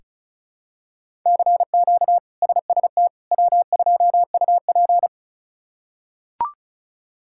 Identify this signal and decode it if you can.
Morse code, heard as a keyed tone, about 34 wpm, 700 Hz. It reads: CQ SST W2UP
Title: K1USN SST QSO Practice - 35wpm